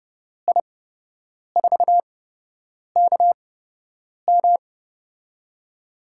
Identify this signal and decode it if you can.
Morse code, heard as a keyed tone, about 32 wpm, spaced as for 8 wpm, 700 Hz. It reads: I4KM
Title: USA Call Signs - 8x Character Spacing - 30wpm